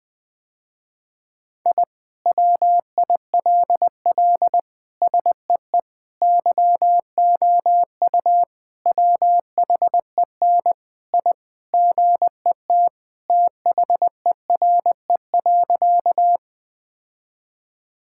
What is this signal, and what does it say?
Morse code, heard as a keyed tone, about 21 wpm, 700 Hz. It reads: I WILL SEE YOU WHEN I GET THERE.
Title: Sentences with 2-Word Phrases - Spoken Beginning - 20wpm